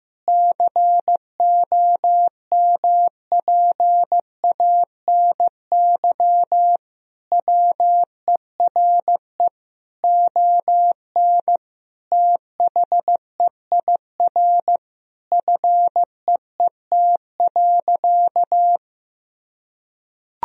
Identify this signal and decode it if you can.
Morse code, heard as a keyed tone, about 15 wpm, 700 Hz. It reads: COMPANY WERE ON THEIR FEET.
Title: Sentences from Top 1000 Words - Review Part 3 - 15wpm